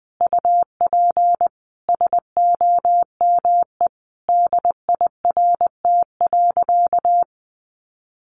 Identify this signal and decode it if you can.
Morse code, heard as a keyed tone, about 19 wpm, 700 Hz. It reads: UP SOME DIRT.
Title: A Sound Of Thunder 20wpm